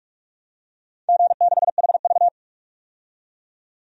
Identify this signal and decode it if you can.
Morse code, heard as a keyed tone, about 44 wpm, 700 Hz. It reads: G6HV
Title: DXCC-100 Call Signs - 4 Characters - 45wpm